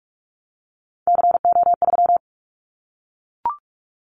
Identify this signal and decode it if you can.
Morse code, heard as a keyed tone, about 44 wpm, 700 Hz. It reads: CO3
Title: Callsign Prefixes for Top 100 DXCC - 45wpm